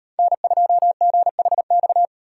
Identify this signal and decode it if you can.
Morse code, heard as a keyed tone, about 38 wpm, 700 Hz. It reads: N2GHX